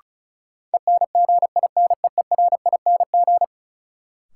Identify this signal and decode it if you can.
Morse code, heard as a keyed tone, about 36 wpm, 700 Hz. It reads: ENGINEERING
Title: Sentences from Top 1500 Words - New Word Drill - 35wpm